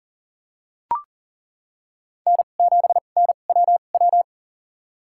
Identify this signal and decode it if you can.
Morse code, heard as a keyed tone, about 38 wpm, spaced as for 28 wpm, 700 Hz. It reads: N7NWW
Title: USA Call Signs - 5 Characters - 2x Character Spacing - 40wpm